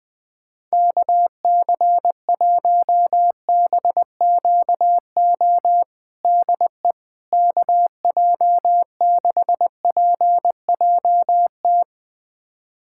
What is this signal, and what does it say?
Morse code, heard as a keyed tone, about 20 wpm, 700 Hz. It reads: KC1BQO DE KJ6PJT
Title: USA Callsign DE Callsign - 6 Characters - 20wpm